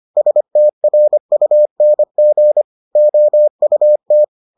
Morse code, heard as a keyed tone, about 25 wpm, 600 Hz. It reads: STRUNG OUT